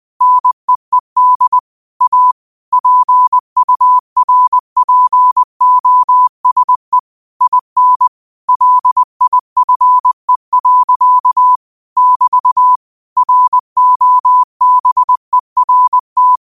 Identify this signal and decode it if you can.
Morse code, heard as a keyed tone, about 20 wpm, 1000 Hz. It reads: NEED A PURPOSE IN LIFE. <BT> ROBERT